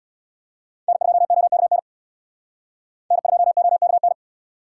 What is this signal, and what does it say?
Morse code, heard as a keyed tone, about 64 wpm, 700 Hz. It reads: N2OKN N2OKN